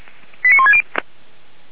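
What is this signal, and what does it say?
Five-tone selective call (ZVEI), heard as quick runs of five tones, tones of about 70 ms. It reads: 89160
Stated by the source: ZVEI five-tone call received on SDR.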